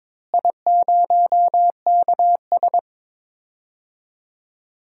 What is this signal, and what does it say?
Morse code, heard as a keyed tone, about 22 wpm, 700 Hz. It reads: I0KS
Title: DXCC-100 Call Signs - 4 Characters - Repeated 2x - 22wpm